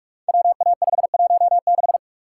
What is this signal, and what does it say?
Morse code, heard as a keyed tone, about 44 wpm, 690 Hz. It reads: WA516
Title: Code Groups - 5 Letter:Numbers - 45wpm